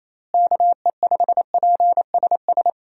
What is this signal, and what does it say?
Morse code, heard as a keyed tone, about 28 wpm, 700 Hz. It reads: KE5PSS
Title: USA Call Signs - 6 Characters - Repeated 3x - 28wpm